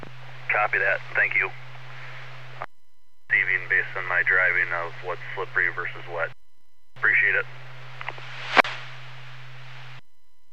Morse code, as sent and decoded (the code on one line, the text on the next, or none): none